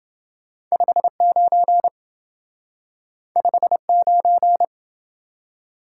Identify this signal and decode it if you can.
Morse code, heard as a keyed tone, about 27 wpm, 700 Hz. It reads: 59 59